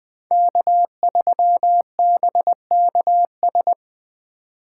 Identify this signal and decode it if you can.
Morse code, heard as a keyed tone, about 20 wpm, 700 Hz. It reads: K3BKS